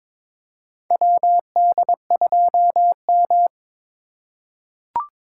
Morse code, heard as a keyed tone, about 23 wpm, 700 Hz. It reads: WD2M
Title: USA Call Signs - 4 Characters - 3x Character Spacing - 22wpm